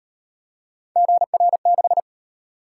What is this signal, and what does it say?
Morse code, heard as a keyed tone, about 37 wpm, 700 Hz. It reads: GR6